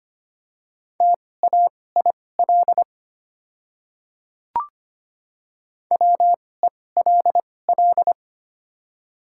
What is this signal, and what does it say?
Morse code, heard as a keyed tone, about 25 wpm, spaced as for 18 wpm, 690 Hz. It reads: TAIL WELL
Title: 2x Character Spacing - Top 500 Words - 25wpm